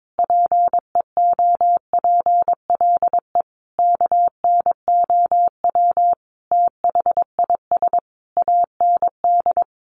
Morse code, heard as a keyed tone, about 22 wpm, 700 Hz. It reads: PEOPLE KNOW THIS AND
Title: Sentences from Top 1500 Words 22wpm